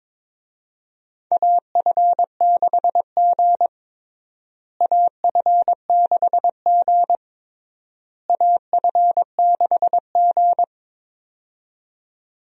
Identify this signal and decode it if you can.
Morse code, heard as a keyed tone, about 22 wpm, 700 Hz. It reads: AF6G AF6G AF6G
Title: USA Call Signs - 4 Characters - Repeated 3x - 22wpm